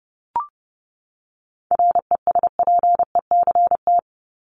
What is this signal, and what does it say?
Morse code, heard as a keyed tone, about 30 wpm, 700 Hz. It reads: RESPECT
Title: Top 1501-1750 Words 30wpm